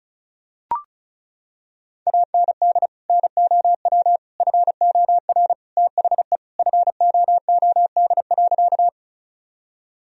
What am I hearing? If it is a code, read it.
Morse code, heard as a keyed tone, about 35 wpm, 700 Hz. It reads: AND NOW FOR THE FOOD.
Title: Sentences from Top 1000 Words - Review Part 1 - 35wpm